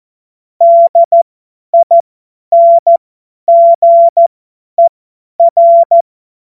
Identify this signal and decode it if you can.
Morse code, heard as a keyed tone, about 14 wpm, spaced as for 10 wpm, 680 Hz. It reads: DINGER
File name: Witz-22-Juli-2025-50-F70